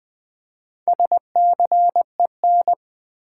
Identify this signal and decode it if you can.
Morse code, heard as a keyed tone, about 20 wpm, 700 Hz. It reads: SCEN
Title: Build A Word - 2 Letters At A Time - 8-Letter Words - 20wpm